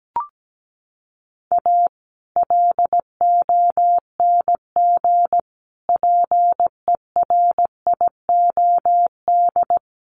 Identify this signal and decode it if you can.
Morse code, heard as a keyed tone, about 17 wpm, 700 Hz. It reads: A LONG PERIOD
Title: Sets of 5 Words 17wpm